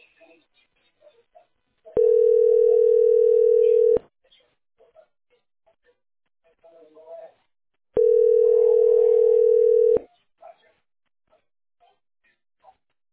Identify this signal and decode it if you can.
Ringback tone (North America), 440 + 480 Hz.